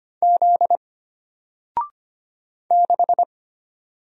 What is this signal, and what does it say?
Morse code, heard as a keyed tone, about 25 wpm, 700 Hz. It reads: Z 6
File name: Single-Letter-Number-25wpm